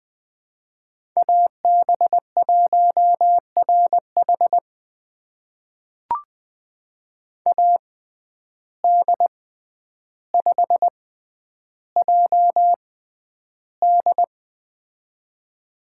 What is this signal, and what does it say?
Morse code, heard as a keyed tone, about 20 wpm, spaced as for 10 wpm, 700 Hz. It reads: AB1RH AD5JD
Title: USA Call Signs - 5 Characters - 6x Character Spacing - 20wpm